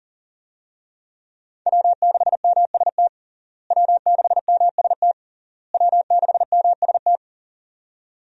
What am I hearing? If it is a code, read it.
Morse code, heard as a keyed tone, about 40 wpm, 700 Hz. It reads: W6MST W6MST W6MST
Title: USA Call Signs - 5 Characters - Repeated 3x - 40wpm